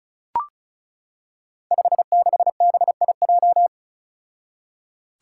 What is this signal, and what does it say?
Morse code, heard as a keyed tone, about 34 wpm, 700 Hz. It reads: 56BIJ
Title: Five Challenging Letter-Number Combinations 35wpm